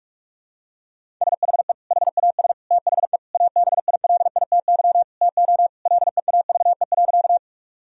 Morse code, heard as a keyed tone, about 46 wpm, 700 Hz. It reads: SHE HAS THE ABILITY TO LEAVE.
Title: Sentences from Top 1000 Words - Review Part 3 - 45wpm